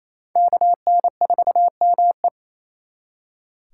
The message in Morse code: KN4ME

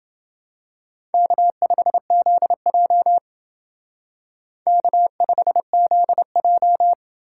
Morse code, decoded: K5ZJ K5ZJ

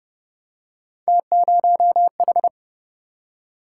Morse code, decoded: T0H

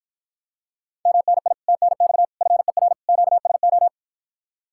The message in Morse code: MNI TNX FER QSO

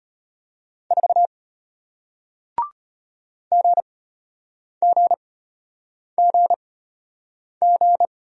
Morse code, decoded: 4 GGGG

